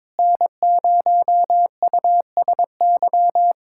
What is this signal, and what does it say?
Morse code, heard as a keyed tone, about 22 wpm, 700 Hz. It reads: N0USY
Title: USA Callsign DE Callsign - 5 Characters - 22wpm